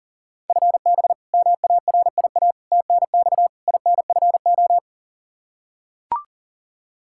Morse code, heard as a keyed tone, about 40 wpm, 690 Hz. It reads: FB MARIA TNX INFO
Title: Practice QSO - Exchange 3 - More Info - 40wpm